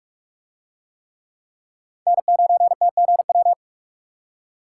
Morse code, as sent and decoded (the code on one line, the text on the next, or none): -. ----. - --. .--
N9TGW